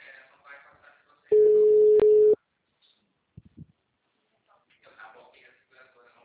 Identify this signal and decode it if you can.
Ringback tone (Europe, 425 Hz), with speech.
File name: out- -2048- - - .58542